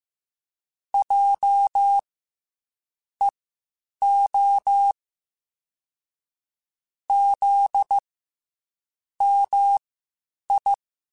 Morse code, heard as a keyed tone, about 15 wpm, spaced as for 7 wpm, 800 Hz. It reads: JEO ZMI
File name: CW Comparisons / CW Comparisons-mp3s / CW Comparisons Broken Characters @13